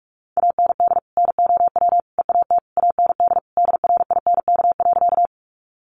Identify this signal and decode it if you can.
Morse code, heard as a keyed tone, about 44 wpm, 700 Hz. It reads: AND NOW EAT AND DRINK.
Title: Sentences from Top 1500 Words 45wpm